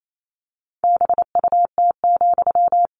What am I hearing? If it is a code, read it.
Morse code, heard as a keyed tone, about 28 wpm, 700 Hz. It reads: BUT,
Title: Sentences from Top 600 Words 28wpm